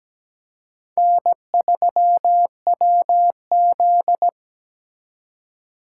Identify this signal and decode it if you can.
Morse code, heard as a keyed tone, about 20 wpm, 700 Hz. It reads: N3WZ
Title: Speed Racing - USA 1x2 Call Signs 17wpm